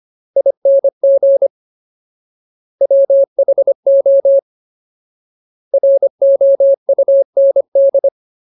Morse code, heard as a keyed tone, about 25 wpm, 550 Hz. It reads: ING WHO ROUND
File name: Ultimate Word List / 25WPM Ultimate